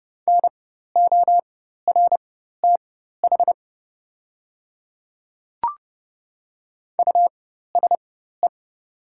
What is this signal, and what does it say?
Morse code, heard as a keyed tone, about 30 wpm, spaced as for 14 wpm, 700 Hz. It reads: NORTH USE
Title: 4x Character Spacing - Top 500 Words - 30wpm